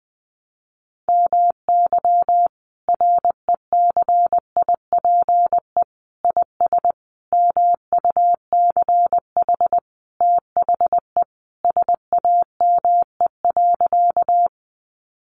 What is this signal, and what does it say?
Morse code, heard as a keyed tone, about 20 wpm, 700 Hz. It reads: MY RECIPE IS MUCH THE SAME.
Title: Sentences from Top 1200 Words 20wpm